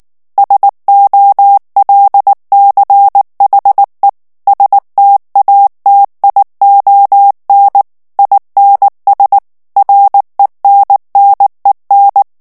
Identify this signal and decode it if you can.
Morse code, heard as a keyed tone, about 19 wpm, 800 Hz. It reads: SOLCHE STATION INS RENNEN
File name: morse mp3 / ALLEMAND texte clair 2 vitesse 18 mots minute